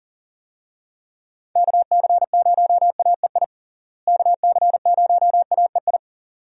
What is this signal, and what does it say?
Morse code, heard as a keyed tone, about 40 wpm, 700 Hz. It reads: KC0AEI KC0AEI
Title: USA Call Signs - 6 Characters - Repeated 2x - 40wpm